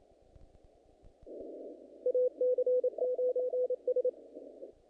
Morse code, heard as a keyed tone, about 28 wpm, 500 Hz. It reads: AC9S
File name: contests / naqp-cw-k5tr-op-k5pi- -2000